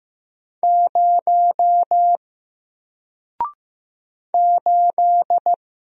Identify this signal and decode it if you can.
Morse code, heard as a keyed tone, about 15 wpm, 700 Hz. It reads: 0 8
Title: Single Letter-Number 15wpm